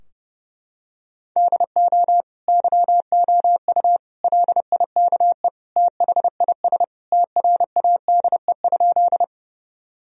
Morse code, heard as a keyed tone, about 32 wpm, 700 Hz. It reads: DO YOU LIKE THIS TRADE?
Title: Sentences from Top 1000 Words - Review Part 4 - 30wpm